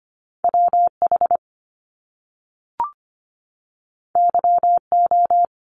Morse code, heard as a keyed tone, about 25 wpm, 700 Hz. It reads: WH YO
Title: Top Two Letter Combinations in Words 25wpm